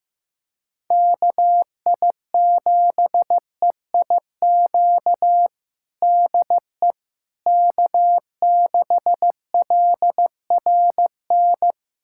Morse code, heard as a keyed tone, about 15 wpm, 700 Hz. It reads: KI7EIQ DE K6LRN
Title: USA Callsign DE Callsign 15wpm